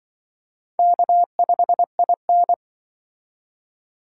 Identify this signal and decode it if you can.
Morse code, heard as a keyed tone, about 25 wpm, 700 Hz. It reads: K5IN
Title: Speed Racing - USA 1x2 Call Signs 20wpm